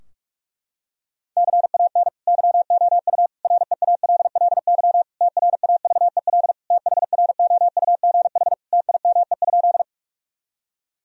Morse code, heard as a keyed tone, about 41 wpm, 700 Hz. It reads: CAN YOU REALLY TRAVEL THROUGH TIME?
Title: Sentences from Top 1000 Words - Review Part 2 - 45wpm